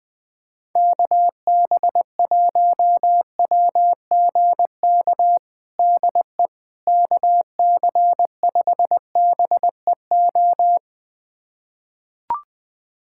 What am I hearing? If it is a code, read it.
Morse code, heard as a keyed tone, about 20 wpm, 700 Hz. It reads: KB1WGK DE KC5BEO
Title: USA Callsign DE Callsign - 6 Characters - 20wpm